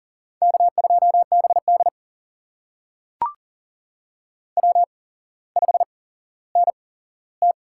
Morse code, heard as a keyed tone, about 38 wpm, spaced as for 17 wpm, 700 Hz. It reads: K2BD W5NT
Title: USA Call Signs - 4 Characters - 8x Character Spacing - 40wpm